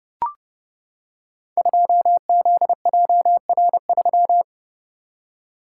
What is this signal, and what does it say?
Morse code, heard as a keyed tone, about 30 wpm, 700 Hz. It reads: 2ZJR3
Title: Code Groups - 5 Letter:Numbers - 30wpm